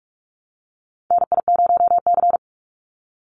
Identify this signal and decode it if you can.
Morse code, heard as a keyed tone, about 44 wpm, 700 Hz. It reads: NI0C
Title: Call Signs Repeated 2x - 45wpm